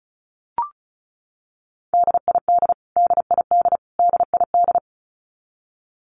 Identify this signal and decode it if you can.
Morse code, heard as a keyed tone, about 34 wpm, 700 Hz. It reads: DID DID DID
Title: Top 500 Words Repeated 35wpm